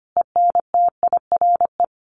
Morse code, heard as a keyed tone, about 25 wpm, 700 Hz. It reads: ENTIRE
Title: A Sound Of Thunder 25wpm